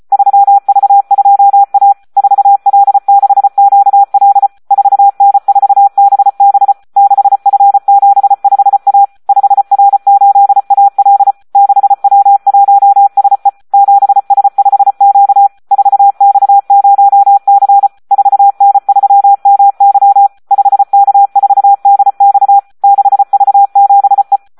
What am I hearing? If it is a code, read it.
Morse code, heard as a keyed tone, about 34 wpm, 800 Hz. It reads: 3V2A 4L6QL 4N4BB 6F75A 5R8AL 6W1SE 7S5Q 4X0C 4N3MY 5K4DX 6V7E
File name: Audio practica CW / SD 1 34